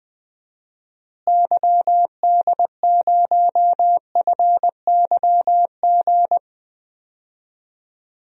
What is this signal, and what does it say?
Morse code, heard as a keyed tone, about 21 wpm, 700 Hz. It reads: YD0FYG
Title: DXCC-100 Call Signs - 6 Characters - Repeated 2x - 20wpm